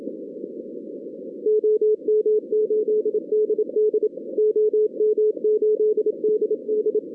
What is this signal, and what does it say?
Morse code, heard as a keyed tone, about 27 wpm, 440 Hz. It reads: OM8DD OM8DD
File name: 2016 arrl dx cw / 2016-ARRL-DX-CW-K5TR- - 0-41